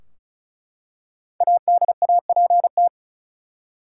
Morse code, heard as a keyed tone, about 36 wpm, 700 Hz. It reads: ADAPT
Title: Five Letter Words - 35wpm